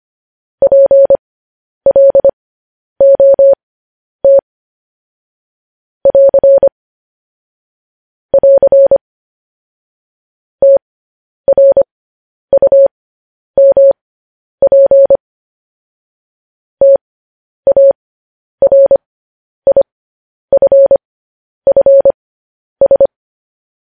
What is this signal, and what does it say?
Morse code, heard as a keyed tone, about 25 wpm, spaced as for 10 wpm, 550 Hz. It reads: PLOT <AR> <AR> TRUMP TARIFFS
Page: news-headlines-in-morse-code-at-10-wpm